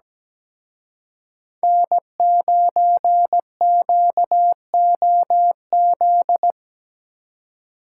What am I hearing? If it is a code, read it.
Morse code, heard as a keyed tone, about 19 wpm, 700 Hz. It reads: N9QOZ
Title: Speed Racing - USA Call Signs 17wpm